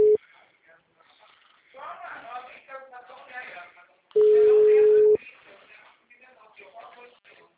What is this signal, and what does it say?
Ringback tone (Europe, 425 Hz), with speech.